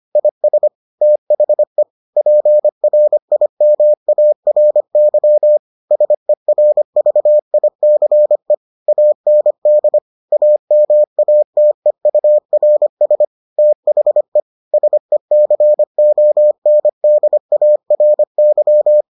Morse code, heard as a keyed tone, about 25 wpm, 600 Hz. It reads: IS THE PRIMARY SERVICE AND AMATEURS THE SECONDARY